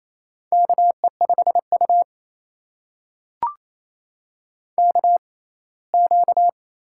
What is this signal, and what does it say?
Morse code, heard as a keyed tone, about 28 wpm, spaced as for 19 wpm, 700 Hz. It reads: KE5U KQ
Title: USA Call Signs - 4 Characters - 6x Character Spacing - 28wpm